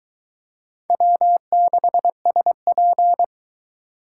Morse code, heard as a keyed tone, about 20 wpm, 700 Hz. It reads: W6SP